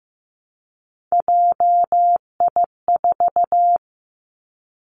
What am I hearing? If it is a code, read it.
Morse code, heard as a keyed tone, about 15 wpm, 700 Hz. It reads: JI4
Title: Three Challenging Letter-Number Combinations 15wpm